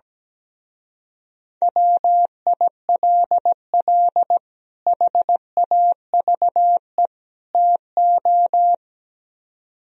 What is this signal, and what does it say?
Morse code, heard as a keyed tone, about 17 wpm, 700 Hz. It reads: WILL HAVE TO
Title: Speed-Racing - Sets of 3 Words 17wpm